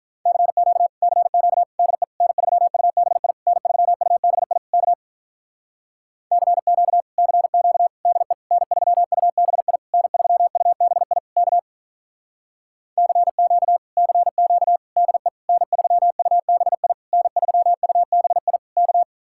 Morse code, heard as a keyed tone, about 44 wpm, 700 Hz. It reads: CQ CQ DE N3UBI N3UBI K CQ CQ DE N3UBI N3UBI K CQ CQ DE N3UBI N3UBI K